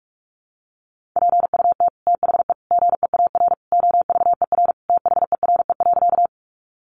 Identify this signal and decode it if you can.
Morse code, heard as a keyed tone, about 45 wpm, 700 Hz. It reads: PUT THE GEAR OVER THERE.
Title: Sentences from Top 1400 Words 45wpm